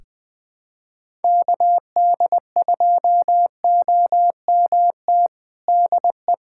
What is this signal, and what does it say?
Morse code, heard as a keyed tone, about 21 wpm, 700 Hz. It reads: KD2OMT DE
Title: USA Callsign DE Callsign 20wpm